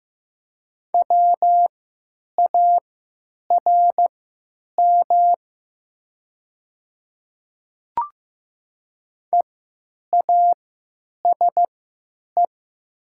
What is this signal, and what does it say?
Morse code, heard as a keyed tone, about 15 wpm, spaced as for 9 wpm, 700 Hz. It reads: WARM EASE
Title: 3x Character Spacing - Top 500 Words - 15wpm